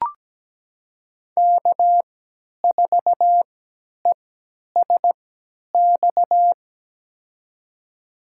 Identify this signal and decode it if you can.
Morse code, heard as a keyed tone, about 17 wpm, spaced as for 10 wpm, 700 Hz. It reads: K4ESX
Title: USA Call Signs - 3x Character Spacing - 17wpm